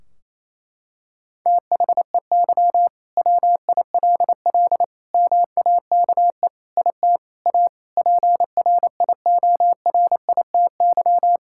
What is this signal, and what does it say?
Morse code, heard as a keyed tone, about 28 wpm, 700 Hz. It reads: THEY WILL MAKE IT A PRIORITY
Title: Sentences from Top 1600 Words 28wpm